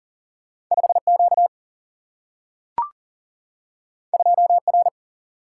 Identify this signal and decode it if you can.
Morse code, heard as a keyed tone, about 41 wpm, 700 Hz. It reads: 5Q 2R
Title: Single-Letter Single-Number 40wpm